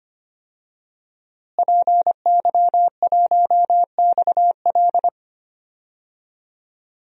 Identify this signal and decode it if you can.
Morse code, heard as a keyed tone, about 25 wpm, 700 Hz. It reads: PY1XL
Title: DXCC-100 Call Signs - 5 Characters - Repeated 2x - 25wpm